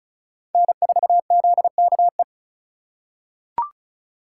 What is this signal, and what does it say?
Morse code, heard as a keyed tone, about 36 wpm, 700 Hz. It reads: N4ZKE
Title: USA Call Signs - 5 Characters - 4x Character Spacing - 35wpm